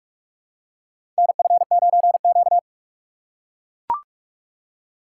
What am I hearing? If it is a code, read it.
Morse code, heard as a keyed tone, about 44 wpm, 700 Hz. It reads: NF9Q
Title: USA Call Signs - 4 Characters - 6x Character Spacing - 45wpm